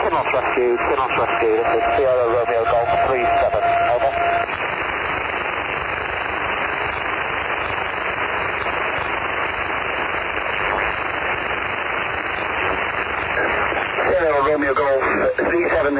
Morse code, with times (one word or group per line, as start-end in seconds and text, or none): none